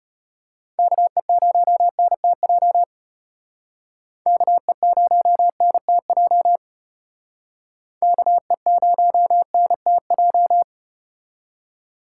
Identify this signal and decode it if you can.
Morse code, heard as a keyed tone, about 33 wpm, 700 Hz. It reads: KE0NTJ KE0NTJ KE0NTJ